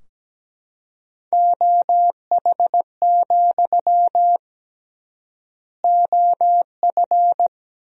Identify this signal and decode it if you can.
Morse code, heard as a keyed tone, about 17 wpm, 700 Hz. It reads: OH, OF
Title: 3x Word-Spacing - Sentences from Top 500 Words - 17wpm